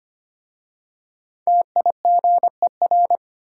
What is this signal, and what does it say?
Morse code, heard as a keyed tone, about 25 wpm, 700 Hz. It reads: TIGER
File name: practice / 25wpm / Easy Words with Hints - 25wpm